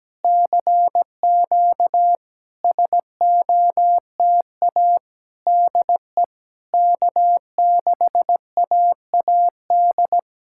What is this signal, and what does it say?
Morse code, heard as a keyed tone, about 17 wpm, 700 Hz. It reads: CQ SOTA DE K6AAD
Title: SOTA QSO Practice 17wpm